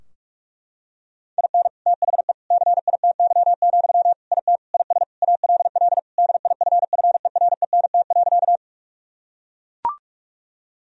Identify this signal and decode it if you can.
Morse code, heard as a keyed tone, about 41 wpm, 700 Hz. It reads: IN THE CITY, IT IS ALL DIFFERENT.
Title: Sentences from Top 1000 Words - Review Part 3 - 45wpm